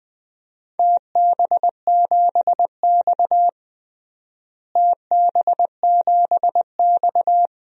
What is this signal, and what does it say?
Morse code, heard as a keyed tone, about 20 wpm, 700 Hz. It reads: TB7X TB7X
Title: DXCC-100 Call Signs - 4 Characters - Repeated 2x - 20wpm